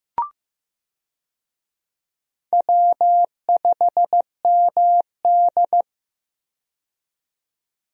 Morse code, heard as a keyed tone, about 15 wpm, 700 Hz. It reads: W5MD
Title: USA Call Signs - 4 Characters - 15wpm